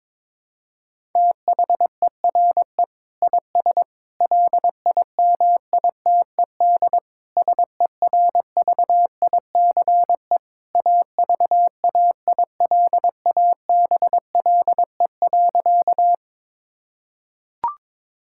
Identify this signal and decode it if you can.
Morse code, heard as a keyed tone, about 22 wpm, 700 Hz. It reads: THERE IS LIMITED SERVICE AVAILABLE.
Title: Sentences from Top 1000 Words - Review Part 6 - 22wpm